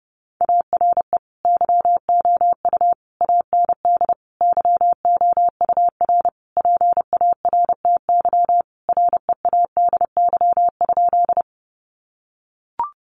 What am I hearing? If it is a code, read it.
Morse code, heard as a keyed tone, about 30 wpm, 700 Hz. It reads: ARE YOU AND YOUR PARTY READY?
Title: Sentences from Top 1400 Words 30wpm